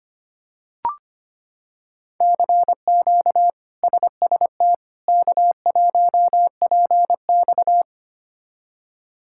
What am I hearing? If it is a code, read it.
Morse code, heard as a keyed tone, about 25 wpm, 700 Hz. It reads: CQ SST K1PX
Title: K1USN SST QSO Practice - 25wpm